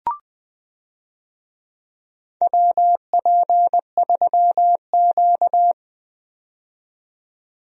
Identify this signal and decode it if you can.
Morse code, heard as a keyed tone, about 20 wpm, 700 Hz. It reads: WP3Q
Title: USA Call Signs - 4 Characters - 20wpm